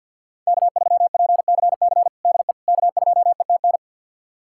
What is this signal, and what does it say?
Morse code, heard as a keyed tone, about 52 wpm, 700 Hz. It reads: K3PCC DE K2ETN
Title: USA Callsign DE Callsign - 5 Characters - 50wpm